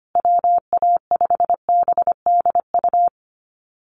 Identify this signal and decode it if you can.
Morse code, heard as a keyed tone, about 25 wpm, 700 Hz. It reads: WA5BDU
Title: Call Signs Repeated 2x - 25wpm